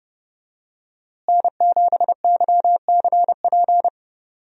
Morse code, heard as a keyed tone, about 30 wpm, 700 Hz. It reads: N7YCP
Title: USA Call Signs - 5 Characters - Repeated 2x - 30wpm